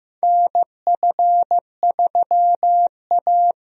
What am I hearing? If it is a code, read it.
Morse code, heard as a keyed tone, about 15 wpm, 700 Hz. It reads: NF3A
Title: USA Call Signs - 4 Characters - Repeated 2x - 15wpm